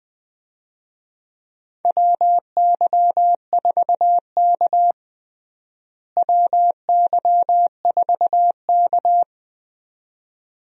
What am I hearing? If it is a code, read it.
Morse code, heard as a keyed tone, about 21 wpm, 700 Hz. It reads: WY4K WY4K
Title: USA Call Signs - 4 Characters - Repeated 2x - 20wpm